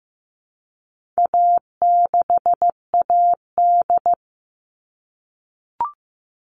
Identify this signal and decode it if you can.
Morse code, heard as a keyed tone, about 15 wpm, 700 Hz. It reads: A6AD
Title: All Cut Numbers - 4-Digit 15wpm